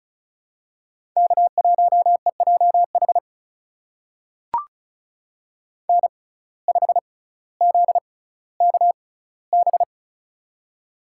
Morse code, heard as a keyed tone, about 35 wpm, spaced as for 18 wpm, 700 Hz. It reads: K1EJH N5ZKB
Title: USA Call Signs - 5 Characters - 6x Character Spacing - 35wpm